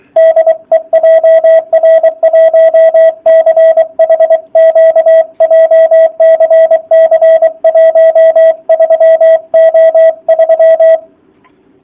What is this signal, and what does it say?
Morse code, heard as a keyed tone, about 23 wpm, 650 Hz, with radio noise. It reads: DE JR1CHQ JCC13O3